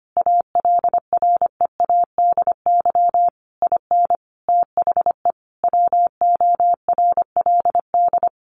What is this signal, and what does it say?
Morse code, heard as a keyed tone, about 25 wpm, 700 Hz. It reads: ALREADY IN THE WORLD
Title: Sentences from Top 1500 Words 25wpm